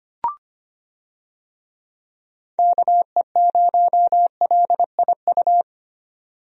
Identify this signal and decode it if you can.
Morse code, heard as a keyed tone, about 25 wpm, 700 Hz. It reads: KE0LIU
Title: USA Call Signs - 6 Characters - 25wpm